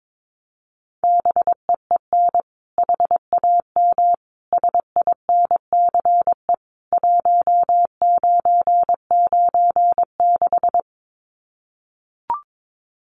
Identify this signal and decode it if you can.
Morse code, heard as a keyed tone, about 22 wpm, 700 Hz. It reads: BEEN HAM SINCE 1996
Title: Common QSO Phrases 22wpm